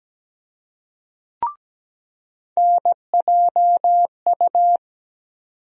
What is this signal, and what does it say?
Morse code, heard as a keyed tone, about 17 wpm, 700 Hz. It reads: NJU
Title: USA Call Sign Postfixes - 3 Letters 17wpm